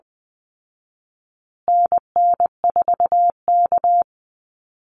Morse code, heard as a keyed tone, about 20 wpm, 700 Hz. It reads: NN4K
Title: Call Signs Repeated 3x - 20wpm